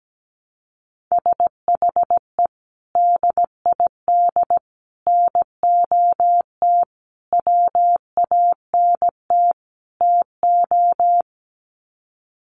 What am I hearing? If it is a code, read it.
Morse code, heard as a keyed tone, about 17 wpm, 700 Hz. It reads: SHE DID NOT WANT TO
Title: Sets of 5 Words 17wpm